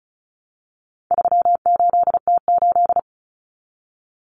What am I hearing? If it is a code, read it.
Morse code, heard as a keyed tone, about 34 wpm, 700 Hz. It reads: 38T8
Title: Common Cut Numbers - 4-Digit 35wpm